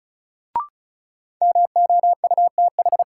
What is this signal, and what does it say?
Morse code, heard as a keyed tone, about 36 wpm, 700 Hz. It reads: MOUTH
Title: Sentences from Top 1500 Words - New Word Drill - 35wpm